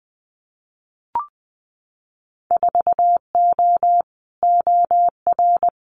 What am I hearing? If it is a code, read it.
Morse code, heard as a keyed tone, about 20 wpm, 700 Hz. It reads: 4O OR
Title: Winter Field Day Exchanges Encore 20wpm